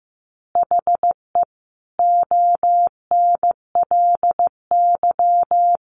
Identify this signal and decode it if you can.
Morse code, heard as a keyed tone, about 15 wpm, 700 Hz. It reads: HE ONLY
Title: Additional Easy Phrases 15wpm